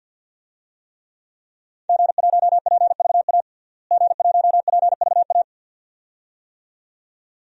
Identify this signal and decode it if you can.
Morse code, heard as a keyed tone, about 48 wpm, 700 Hz. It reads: G1PVA G1PVA
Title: DXCC-100 Call Signs - 5 Characters - Repeated 2x - 50wpm